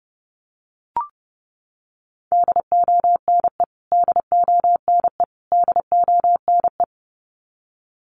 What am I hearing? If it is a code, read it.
Morse code, heard as a keyed tone, about 30 wpm, 700 Hz. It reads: DONE DONE DONE
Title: Top 500 Words Repeated 30wpm